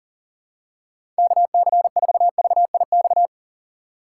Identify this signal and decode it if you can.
Morse code, heard as a keyed tone, about 40 wpm, 700 Hz. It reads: KC4VIX
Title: USA Call Signs - 6 Characters - 2x Character Spacing - 40wpm